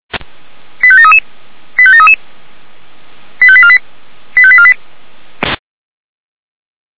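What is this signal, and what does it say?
Five-tone selective call (ZVEI), heard as quick runs of five tones, tones of about 70 ms. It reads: 85733 85733 85848 85848
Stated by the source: ZVEI five-tone selective call 89233 received by SDR, from the feuerwehr-orlamuende.org audio archive (fire brigade Orlamünde).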